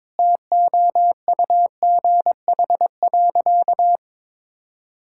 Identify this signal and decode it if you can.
Morse code, heard as a keyed tone, about 22 wpm, 700 Hz. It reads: TOUGH.
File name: Sentences with 7-Word Phrases - 22wpm